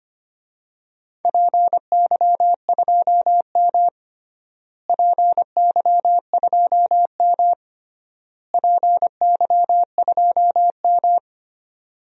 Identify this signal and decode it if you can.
Morse code, heard as a keyed tone, about 25 wpm, 700 Hz. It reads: PY2M PY2M PY2M
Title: DXCC-100 Call Signs - 4 Characters - Repeated 3x - 25wpm